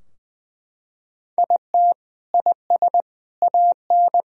Morse code, heard as a keyed tone, about 20 wpm, 700 Hz. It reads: IT IS AN
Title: Sentences from Top 1000 Words - Review Part 4 - 20wpm